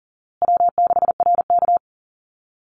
Morse code, heard as a keyed tone, about 40 wpm, 700 Hz. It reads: W6RK
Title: Call Signs Repeated 3x - 40wpm